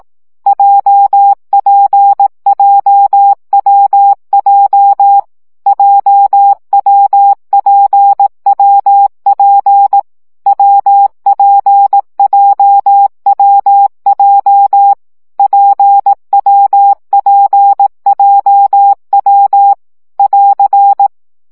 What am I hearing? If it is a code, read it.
Morse code, heard as a keyed tone, about 18 wpm, 800 Hz. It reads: JPJWJ JWPWP WPJWJ PWPJW <AR>
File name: CW gyakorlas w p j 15 baud